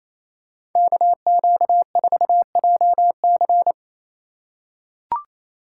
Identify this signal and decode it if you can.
Morse code, heard as a keyed tone, about 28 wpm, 700 Hz. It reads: KQ4JC
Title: USA Call Signs - 2x Character Spacing - 28wpm